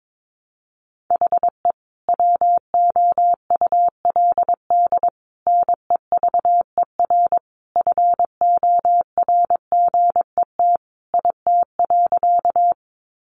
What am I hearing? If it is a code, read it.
Morse code, heard as a keyed tone, about 22 wpm, 700 Hz. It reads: HE WOULD NEVER FORGET IT.
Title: Sentences from Top 1200 Words 22wpm